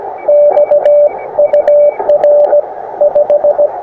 Morse code, heard as a keyed tone, about 17 wpm, 590 Hz, with radio noise. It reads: XUR 5
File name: attachments / us7ign a 20m